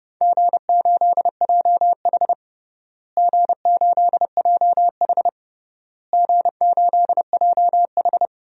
Morse code, heard as a keyed tone, about 30 wpm, 700 Hz. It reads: G8JH G8JH G8JH
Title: DXCC-100 Call Signs - 4 Characters - Repeated 3x - 30wpm